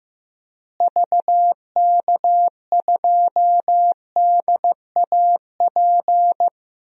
Morse code, heard as a keyed tone, about 16 wpm, 700 Hz. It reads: VK2DAP
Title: DXCC-100 Call Signs - 6 Characters - Repeated 3x - 15wpm